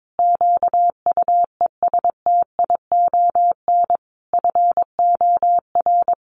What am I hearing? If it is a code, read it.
Morse code, heard as a keyed tone, about 22 wpm, 700 Hz. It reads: QUESTION FOR
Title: Sentences from Top 1500 Words 22wpm